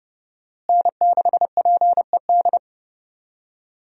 Morse code, heard as a keyed tone, about 30 wpm, 700 Hz. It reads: N6PED
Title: USA Call Signs - 5 Characters - Repeated 2x - 30wpm